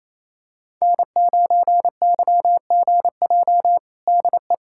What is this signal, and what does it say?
Morse code, heard as a keyed tone, about 27 wpm, 700 Hz. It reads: N9YGJ DE
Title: USA Callsign DE Callsign 28wpm